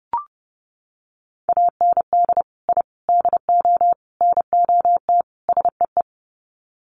Morse code, heard as a keyed tone, about 30 wpm, 700 Hz. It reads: AND I DO NOT SEE
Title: Sets of 5 Words Encore2 30wpm